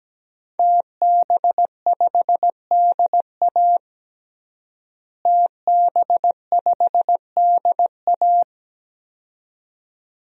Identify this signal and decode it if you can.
Morse code, heard as a keyed tone, about 17 wpm, 700 Hz. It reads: TB5DA TB5DA
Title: DXCC-100 Call Signs - 5 Characters - Repeated 2x - 17wpm